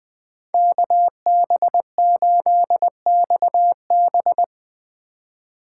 Morse code, heard as a keyed tone, about 23 wpm, 700 Hz. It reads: KB8XB
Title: Speed Racing - USA Call Signs 20wpm